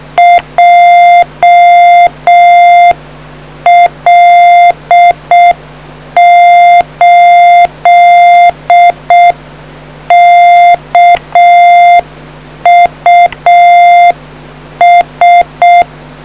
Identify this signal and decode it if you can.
Morse code, heard as a keyed tone, about 6 wpm, 700 Hz. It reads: JL8KUS